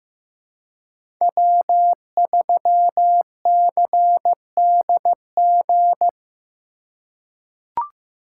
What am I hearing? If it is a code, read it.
Morse code, heard as a keyed tone, about 15 wpm, 700 Hz. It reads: W3CDG
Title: USA Call Signs - 3x Character Spacing - 15wpm